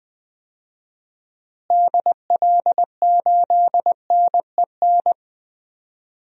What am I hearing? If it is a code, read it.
Morse code, heard as a keyed tone, about 20 wpm, 700 Hz. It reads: DL8NEN
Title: DXCC-100 Call Signs - 6 Characters - Repeated 3x - 20wpm